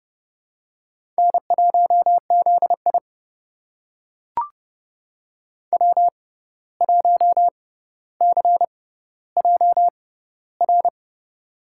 Morse code, heard as a keyed tone, about 30 wpm, spaced as for 14 wpm, 700 Hz. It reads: N1ZI W1CJR
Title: USA Call Signs - 6x Character Spacing - 30wpm